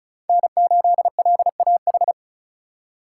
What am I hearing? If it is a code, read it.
Morse code, heard as a keyed tone, about 34 wpm, 700 Hz. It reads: N8LAH
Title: USA Call Signs - 5 Characters - 6x Character Spacing - 35wpm